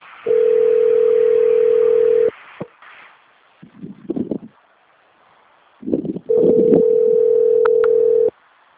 Ringback tone (North America), 440 + 480 Hz.